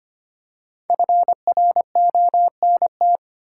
Morse code, heard as a keyed tone, about 25 wpm, 700 Hz. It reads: FRONT
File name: 5 Letter Words - First 3 Letters Given - 25wpm